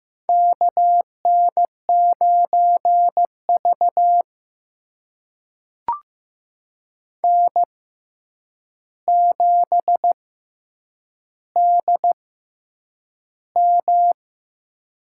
Morse code, heard as a keyed tone, about 15 wpm, spaced as for 8 wpm, 700 Hz. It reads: KN9V N7DM
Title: USA Call Signs - 4 Characters - 6x Character Spacing - 15wpm